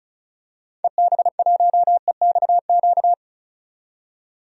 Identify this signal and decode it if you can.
Morse code, heard as a keyed tone, about 36 wpm, 700 Hz. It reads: EB1EXQ
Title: DXCC-100 Call Signs - 6 Characters - 35wpm